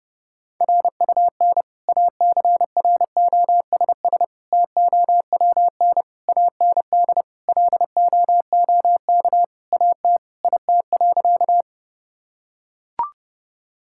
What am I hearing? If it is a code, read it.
Morse code, heard as a keyed tone, about 31 wpm, 700 Hz. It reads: RUN ACROSS TOWN AND LOOK AT IT.
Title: Sentences from Top 1000 Words - Review Part 5 - 30wpm